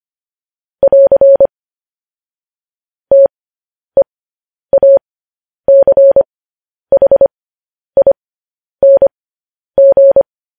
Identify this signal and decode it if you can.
Morse code, heard as a keyed tone, about 25 wpm, spaced as for 10 wpm, 550 Hz. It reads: <AR> TEACHING
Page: news-headlines-in-morse-code-at-10-wpm